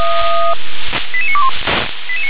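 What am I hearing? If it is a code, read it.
Five-tone selective call (ZVEI), heard as quick runs of five tones, tones of about 70 ms. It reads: 88921
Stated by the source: ZVEI five-tone selective call 88955, received by SDR.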